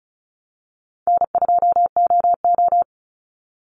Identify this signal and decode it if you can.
Morse code, heard as a keyed tone, about 34 wpm, 700 Hz. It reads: N2OO
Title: Call Signs - 35wpm